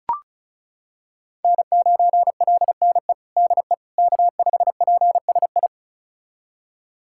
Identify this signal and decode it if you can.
Morse code, heard as a keyed tone, about 36 wpm, 700 Hz. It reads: N9LNE DE K5PSI
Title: USA Callsign DE Callsign - 5 Characters - 35wpm